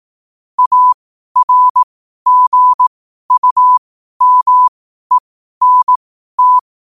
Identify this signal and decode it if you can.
Morse code, heard as a keyed tone, about 18 wpm, spaced as for 12 wpm, 1000 Hz. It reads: ARGUMENT
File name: qotdfiles / 12wpm